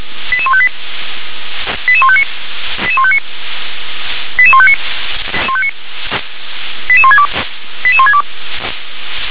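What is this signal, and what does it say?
Five-tone selective call (ZVEI), heard as quick runs of five tones, tones of about 70 ms. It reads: 88158 88159 88159 88159 88162 88162
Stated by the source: ZVEI five-tone call 88119 received by SDR.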